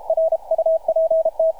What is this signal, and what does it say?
Morse code, heard as a keyed tone, about 40 wpm, 590 Hz, with radio noise. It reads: RUPT
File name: archived / CRSnet